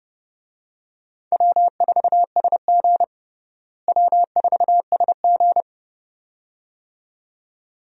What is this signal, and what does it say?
Morse code, heard as a keyed tone, about 30 wpm, 700 Hz. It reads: W4SG W4SG
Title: USA Call Signs - 4 Characters - Repeated 2x - 30wpm